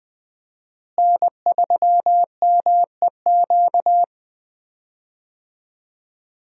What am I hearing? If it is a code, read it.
Morse code, heard as a keyed tone, about 20 wpm, 700 Hz. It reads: N3MEQ
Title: USA Call Signs - 5 Characters - Repeated 2x - 20wpm